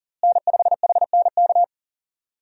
Morse code, heard as a keyed tone, about 40 wpm, 700 Hz. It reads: N5HNK
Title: USA Call Signs - 5 Characters - Repeated 2x - 40wpm